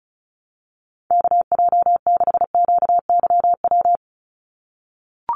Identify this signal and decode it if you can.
Morse code, heard as a keyed tone, about 34 wpm, 700 Hz. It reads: KJ6QYW
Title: USA Call Signs 35wpm